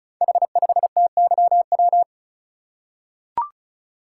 Morse code, heard as a keyed tone, about 36 wpm, 690 Hz. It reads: H5TYW
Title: Code Groups - 5 Letter:Numbers - 35wpm